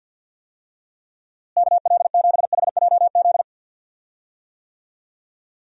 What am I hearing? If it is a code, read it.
Morse code, heard as a keyed tone, about 47 wpm, 700 Hz. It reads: KL7HJZ
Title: DXCC-100 Call Signs - 6 Characters - Repeated 2x - 50wpm